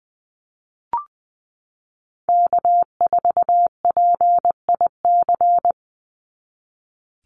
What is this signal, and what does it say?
Morse code, heard as a keyed tone, about 20 wpm, 700 Hz. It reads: K4PIC
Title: NA SOTA Chaser Callsigns 20wpm